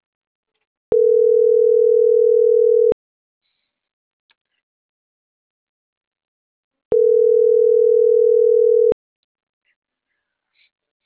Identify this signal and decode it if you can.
Ringback tone (North America), 440 + 480 Hz.